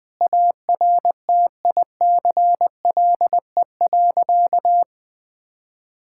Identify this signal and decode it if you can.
Morse code, heard as a keyed tone, about 20 wpm, 700 Hz. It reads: ARTICLE.
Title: Sentences from Top 1000 Words - Review Part 5 - 20wpm